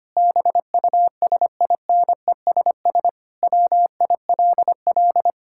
Morse code, heard as a keyed tone, about 25 wpm, 700 Hz. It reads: BUSINESS WILL
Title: Sentences from Top 1000 Words - Review Part 5 - 25wpm